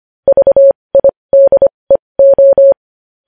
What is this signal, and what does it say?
Morse code, heard as a keyed tone, about 25 wpm, spaced as for 20 wpm, 550 Hz. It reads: VIDEO
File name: podcast / News headlines-1x-25x20